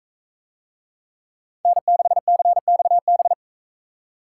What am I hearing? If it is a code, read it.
Morse code, heard as a keyed tone, about 41 wpm, 700 Hz. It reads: N6CXB